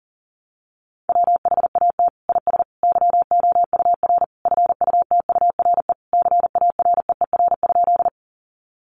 Morse code, heard as a keyed tone, about 37 wpm, 700 Hz. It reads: WHAT IS YOUR FUTURE CAREER?
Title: Sentences from Top 1000 Words 40wpm